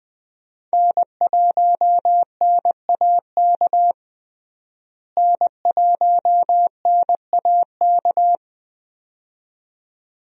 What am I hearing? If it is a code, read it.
Morse code, heard as a keyed tone, about 20 wpm, 700 Hz. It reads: N1NAK N1NAK
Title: USA Call Signs - 5 Characters - Repeated 2x - 20wpm